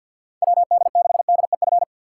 Morse code, heard as a keyed tone, about 52 wpm, 700 Hz. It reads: WD6DEF
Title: USA Call Signs - 6 Characters - 3x Character Spacing - 50wpm